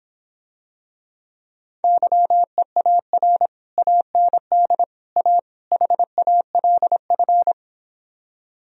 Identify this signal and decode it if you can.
Morse code, heard as a keyed tone, about 26 wpm, 700 Hz. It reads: YEAR AND A HALF